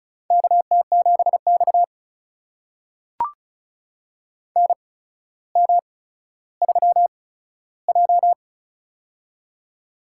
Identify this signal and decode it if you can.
Morse code, heard as a keyed tone, about 36 wpm, spaced as for 15 wpm, 700 Hz. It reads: KT7X NM3J
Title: USA Call Signs - 4 Characters - 8x Character Spacing - 35wpm